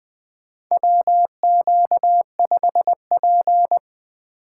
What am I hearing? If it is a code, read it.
Morse code, heard as a keyed tone, about 20 wpm, 700 Hz. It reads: WQ5P
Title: USA Call Signs - 4 Characters - 3x Character Spacing - 20wpm